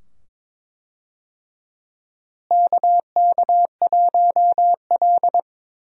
Morse code, heard as a keyed tone, about 22 wpm, 700 Hz. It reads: KK1L
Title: USA Call Signs - 4 Characters - Repeated 2x - 22wpm